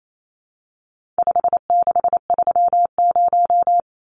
Morse code, heard as a keyed tone, about 28 wpm, 700 Hz. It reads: 5630
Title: Numbers - 4-Digit 28wpm